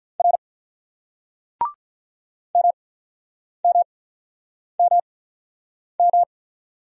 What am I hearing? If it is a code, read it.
Morse code, heard as a keyed tone, about 48 wpm, spaced as for 9 wpm, 700 Hz. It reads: R MMMM